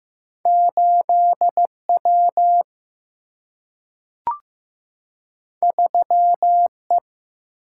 Morse code, heard as a keyed tone, about 15 wpm, 700 Hz. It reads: 8 W 3 E
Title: Single-Letter Single-Number 15wpm